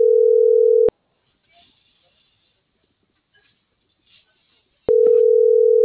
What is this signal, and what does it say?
Ringback tone (North America), 440 + 480 Hz.